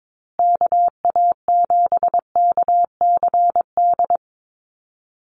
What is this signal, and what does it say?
Morse code, heard as a keyed tone, about 22 wpm, 700 Hz. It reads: KA7KCD
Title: USA Call Signs Repeated 3x 22wpm